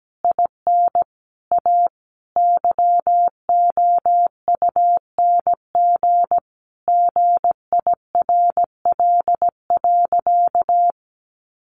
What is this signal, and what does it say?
Morse code, heard as a keyed tone, about 17 wpm, 700 Hz. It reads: IN A YOUNG GIRL.
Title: Sentences from Top 1400 Words 17wpm